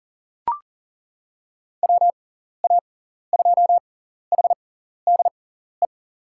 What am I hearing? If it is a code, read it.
Morse code, heard as a keyed tone, about 40 wpm, spaced as for 14 wpm, 700 Hz. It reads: WA2HDE
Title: USA Call Signs - 6 Characters - 6x Character Spacing - 40wpm